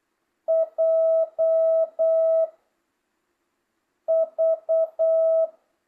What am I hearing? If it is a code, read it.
Morse code, heard as a keyed tone, about 8 wpm, 650 Hz. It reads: J V